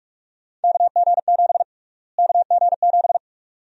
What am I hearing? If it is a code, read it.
Morse code, heard as a keyed tone, about 42 wpm, 700 Hz. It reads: KG7 KG7
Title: USA Call Sign Prefixes - Repeated 2x - 45wpm